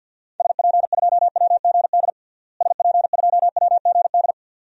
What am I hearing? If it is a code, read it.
Morse code, heard as a keyed tone, about 48 wpm, 700 Hz. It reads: SP2WGD SP2WGD
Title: DXCC-100 Call Signs - 6 Characters - Repeated 2x - 50wpm